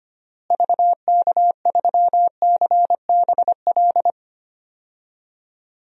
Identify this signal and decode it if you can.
Morse code, heard as a keyed tone, about 25 wpm, 700 Hz. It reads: VK3CBL
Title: DXCC-100 Call Signs - 6 Characters - Repeated 2x - 25wpm